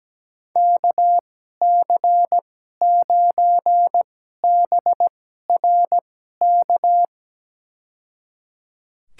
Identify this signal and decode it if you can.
Morse code, heard as a keyed tone, about 17 wpm, spaced as for 12 wpm, 700 Hz. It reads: KC9BRK
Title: USA Call Signs - 6 Characters - 2x Character Spacing - 17wpm